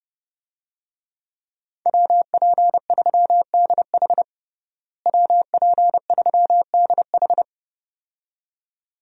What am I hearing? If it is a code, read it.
Morse code, heard as a keyed tone, about 30 wpm, 700 Hz. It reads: WP3DH WP3DH
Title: USA Call Signs - 5 Characters - Repeated 2x - 30wpm